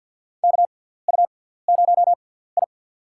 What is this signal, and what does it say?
Morse code, heard as a keyed tone, about 49 wpm, spaced as for 17 wpm, 700 Hz. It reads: KU0I
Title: USA Call Signs - 4 Characters - 6x Character Spacing - 50wpm